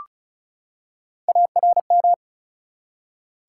Morse code, heard as a keyed tone, about 36 wpm, 700 Hz. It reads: ARM
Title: Three Letter Words - 35wpm